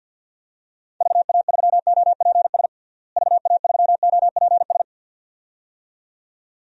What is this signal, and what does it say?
Morse code, heard as a keyed tone, about 49 wpm, 700 Hz. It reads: VA3OPS VA3OPS
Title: DXCC-100 Call Signs - 6 Characters - Repeated 2x - 50wpm